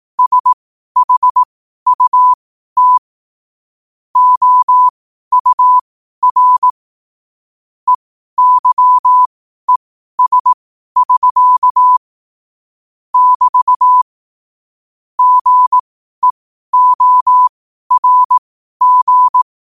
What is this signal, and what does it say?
Morse code, heard as a keyed tone, about 18 wpm, spaced as for 12 wpm, 1000 Hz. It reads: SHUT OUR EYES<SK> <BT> GEORG
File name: qotdfiles / 12wpm